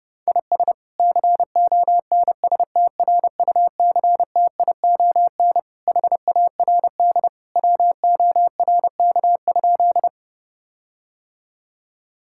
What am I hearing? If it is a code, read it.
Morse code, heard as a keyed tone, about 30 wpm, 700 Hz. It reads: IS CONSTRUCTION HARD WORK?
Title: Sentences from Top 1600 Words 30wpm